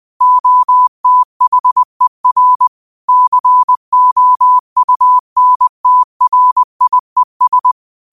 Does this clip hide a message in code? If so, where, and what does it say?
Yes: Morse code from 0.2 to 7.7 s, heard as a keyed tone, about 20 wpm, 1000 Hz. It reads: OTHER COUNTRIES